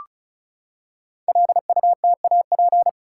Morse code, heard as a keyed tone, about 36 wpm, 690 Hz. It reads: LUTAP
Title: Code Groups - 5 Letters - 35wpm